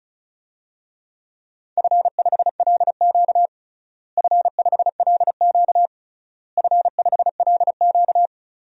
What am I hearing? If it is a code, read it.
Morse code, heard as a keyed tone, about 36 wpm, 700 Hz. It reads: F5LQ F5LQ F5LQ
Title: DXCC-100 Call Signs - 4 Characters - Repeated 3x - 35wpm